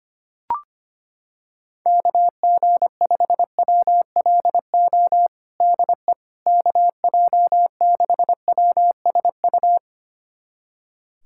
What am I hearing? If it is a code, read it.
Morse code, heard as a keyed tone, about 25 wpm, 700 Hz. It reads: KG5WLO DE KJ6WSU
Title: USA Callsign DE Callsign 25wpm